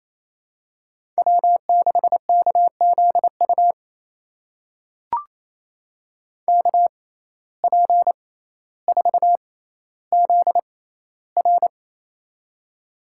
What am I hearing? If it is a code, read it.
Morse code, heard as a keyed tone, about 28 wpm, spaced as for 14 wpm, 700 Hz. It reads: W6KZU KP4ZR
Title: USA Call Signs - 5 Characters - 6x Character Spacing - 28wpm